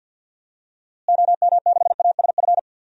Morse code, heard as a keyed tone, about 51 wpm, 700 Hz. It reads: OM6ASF
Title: DXCC-100 Call Signs - 6 Characters - Repeated 3x - 50wpm